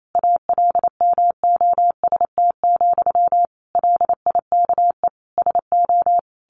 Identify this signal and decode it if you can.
Morse code, heard as a keyed tone, about 28 wpm, 700 Hz. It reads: ALMOST, LIKE SO
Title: A Sound Of Thunder 28wpm